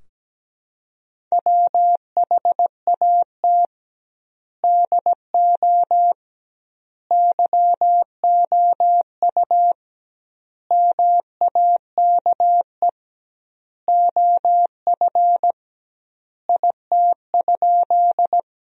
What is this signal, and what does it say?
Morse code, heard as a keyed tone, about 17 wpm, 700 Hz. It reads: WHAT DO YOU MAKE OF IT?
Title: 2x Word-Spacing - Sentences from Top 100 Words - 17wpm